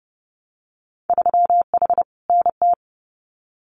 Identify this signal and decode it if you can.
Morse code, heard as a keyed tone, about 30 wpm, 700 Hz. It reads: 3H NT
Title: Winter Field Day Exchanges Encore 30wpm